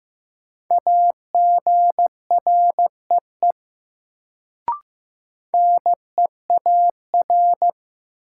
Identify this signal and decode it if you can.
Morse code, heard as a keyed tone, about 15 wpm, 700 Hz. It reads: AGREE NEAR
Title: Sets of 5 Words Encore - New Word Drill - 15wpm